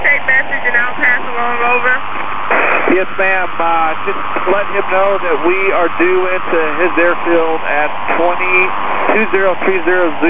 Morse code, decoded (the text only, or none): none